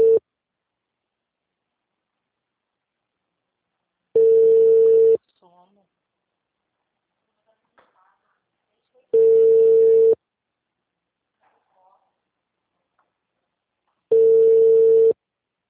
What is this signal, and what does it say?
Ringback tone (China, 450 Hz), with speech.
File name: out- -2007- - - .18